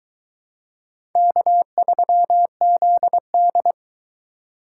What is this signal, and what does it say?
Morse code, heard as a keyed tone, about 23 wpm, 700 Hz. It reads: K3ZD